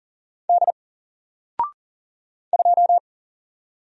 Morse code, heard as a keyed tone, about 39 wpm, 700 Hz. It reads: D 2
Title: Single Letter-Number 40wpm